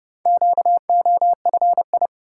Morse code, heard as a keyed tone, about 29 wpm, 700 Hz. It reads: QOFI